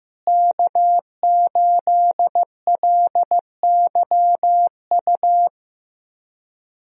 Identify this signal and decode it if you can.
Morse code, heard as a keyed tone, about 15 wpm, 700 Hz. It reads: K8LYU
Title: USA Call Signs - 5 Characters - Repeated 3x - 15wpm